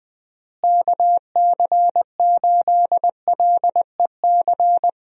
Morse code, heard as a keyed tone, about 21 wpm, 700 Hz. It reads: KC8LEC
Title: USA Call Signs - 6 Characters - Repeated 3x - 20wpm